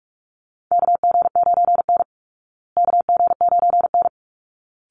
Morse code, heard as a keyed tone, about 44 wpm, 700 Hz. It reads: KG9N KG9N
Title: Call Signs Repeated 3x - 45wpm